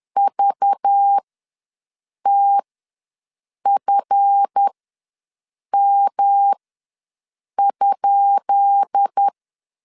Morse code, heard as a keyed tone, about 10 wpm, spaced as for 6 wpm, 800 Hz. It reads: VTFM?
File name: LEZIONI / 10 SIGNS / 10B INT ws 10wpm 4134